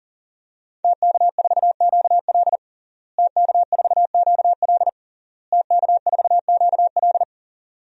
Morse code, heard as a keyed tone, about 40 wpm, 700 Hz. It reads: TK4QL TK4QL TK4QL
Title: DXCC-100 Call Signs - 5 Characters - Repeated 3x - 40wpm